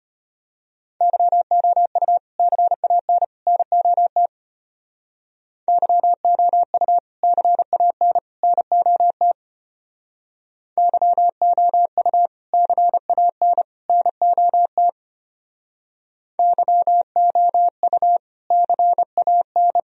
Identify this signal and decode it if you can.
Morse code, heard as a keyed tone, about 30 wpm, 700 Hz. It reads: YOU CAN NOT YOU CAN NOT YOU CAN NOT YOU CAN
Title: Speed-Racing - Sets of 3 Words 25wpm